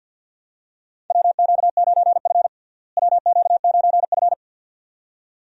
Morse code, heard as a keyed tone, about 52 wpm, 700 Hz. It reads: WQ9F WQ9F
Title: USA Call Signs - 4 Characters - Repeated 2x - 50wpm